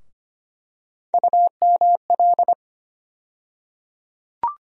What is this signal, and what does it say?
Morse code, heard as a keyed tone, about 25 wpm, 700 Hz. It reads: UML
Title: USA Call Sign Postfixes - 3 Letters 25wpm